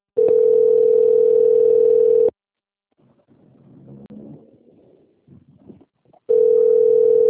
Ringback tone (North America), 440 + 480 Hz.
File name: out- - - - - .97338